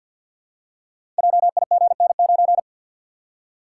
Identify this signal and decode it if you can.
Morse code, heard as a keyed tone, about 48 wpm, 700 Hz. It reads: JIGN9
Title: Code Groups - 5 Letter:Numbers - 50wpm